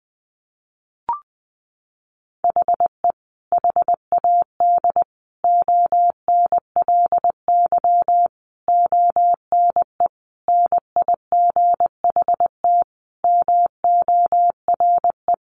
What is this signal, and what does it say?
Morse code, heard as a keyed tone, about 20 wpm, 700 Hz. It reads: HE HAD ONLY ONE NIGHT MORE
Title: Sentences from Top 300 Words 20wpm